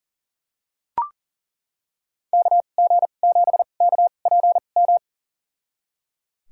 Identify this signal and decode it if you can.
Morse code, heard as a keyed tone, about 38 wpm, spaced as for 28 wpm, 700 Hz. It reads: KG7KPM
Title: USA Call Signs - 6 Characters - 2x Character Spacing - 40wpm